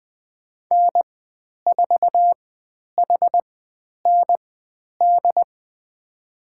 Morse code, heard as a keyed tone, about 20 wpm, spaced as for 10 wpm, 700 Hz. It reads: N4HND
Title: 102 Lesson 17g - Letter u - Call Signs 20wpm